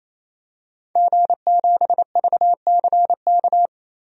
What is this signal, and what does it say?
Morse code, heard as a keyed tone, about 27 wpm, 700 Hz. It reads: G7VCK
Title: DXCC-100 Call Signs - 5 Characters - 28wpm